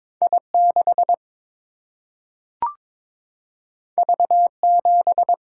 Morse code, heard as a keyed tone, about 22 wpm, 700 Hz. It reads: I6 V7
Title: Single-Letter Single-Number 22wpm